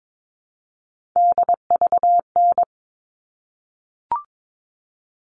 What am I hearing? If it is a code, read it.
Morse code, heard as a keyed tone, about 22 wpm, 700 Hz. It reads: DVN
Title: All Cut Numbers - 3-Digit 22wpm